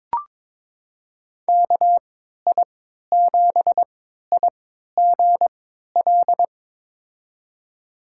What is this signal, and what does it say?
Morse code, heard as a keyed tone, about 21 wpm, spaced as for 12 wpm, 700 Hz. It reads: KI7IGL
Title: USA Call Signs - 6 Characters - 3x Character Spacing - 22wpm